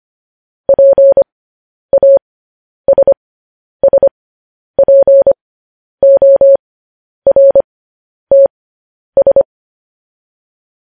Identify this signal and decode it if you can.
Morse code, heard as a keyed tone, about 25 wpm, spaced as for 10 wpm, 550 Hz. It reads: PASSPORTS